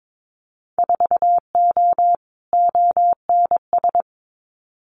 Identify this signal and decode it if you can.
Morse code, heard as a keyed tone, about 22 wpm, 700 Hz. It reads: 4O ONS
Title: Winter Field Day Exchanges 22wpm